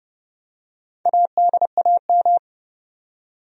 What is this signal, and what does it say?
Morse code, heard as a keyed tone, about 30 wpm, 700 Hz. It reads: ADAM
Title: USA Common Names - Max 4 Letters - 30wpm